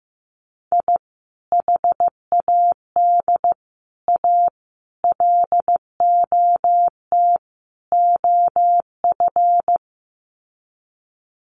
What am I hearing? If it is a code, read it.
Morse code, heard as a keyed tone, about 15 wpm, 700 Hz. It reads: I HAD A LOT OF
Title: Sets of 5 Words Encore 15wpm